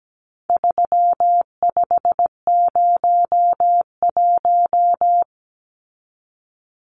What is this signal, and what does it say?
Morse code, heard as a keyed tone, about 17 wpm, 700 Hz. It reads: 3501
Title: Numbers - 4-Digit 17wpm